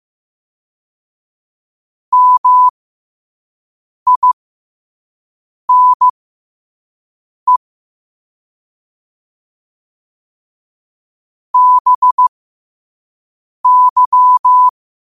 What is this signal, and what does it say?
Morse code, heard as a keyed tone, about 15 wpm, spaced as for 5 wpm, 1000 Hz. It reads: MINE BY
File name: qotdfiles / 5wpm